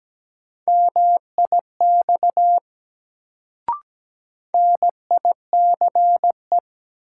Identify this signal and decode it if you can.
Morse code, heard as a keyed tone, about 17 wpm, 700 Hz. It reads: MIX NICE
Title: Sentences from Top 1100 Words - New Word Drill Alt - 17wpm